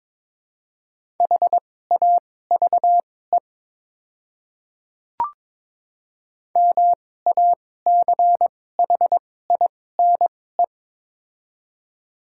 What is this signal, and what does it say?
Morse code, heard as a keyed tone, about 21 wpm, spaced as for 16 wpm, 700 Hz. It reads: HAVE MACHINE
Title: 2x Character Spacing - Top 500 Words - 22wpm